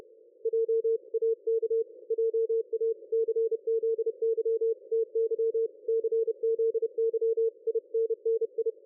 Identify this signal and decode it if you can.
Morse code, heard as a keyed tone, about 31 wpm, 450 Hz. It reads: JAK JA CZY TY CZY INNI
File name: dlazaawansowanych / nagraniazpasma / ExpertSDR2 line out 23.01.2020 14.38.27